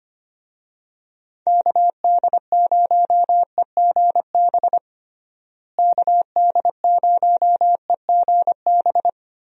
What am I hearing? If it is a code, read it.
Morse code, heard as a keyed tone, about 25 wpm, 700 Hz. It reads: KD0EGB KD0EGB
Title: USA Call Signs - 6 Characters - Repeated 2x - 25wpm